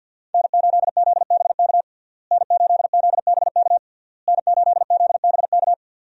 Morse code, heard as a keyed tone, about 52 wpm, 700 Hz. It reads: N8ZBK N8ZBK N8ZBK
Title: USA Call Signs - 5 Characters - Repeated 3x - 50wpm